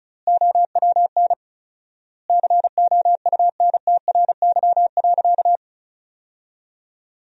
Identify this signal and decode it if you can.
Morse code, heard as a keyed tone, about 34 wpm, 700 Hz. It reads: OWN COUNTRY.
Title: 4x Word-Spacing - Sentences from Top 300 Words - 35wpm